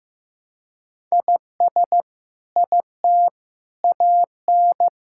Morse code, heard as a keyed tone, about 15 wpm, 700 Hz. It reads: IS IT AN
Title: Sentences from Top 1000 Words - Review Part 5 - 15wpm